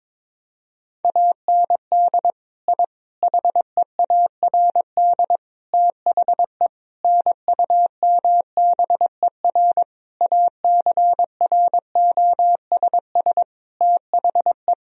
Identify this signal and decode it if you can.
Morse code, heard as a keyed tone, about 22 wpm, 700 Hz. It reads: AND I HEARD THE NUMBER ACROSS THE
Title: Sentences from Top 1000 Words - Review Part 1 - 22wpm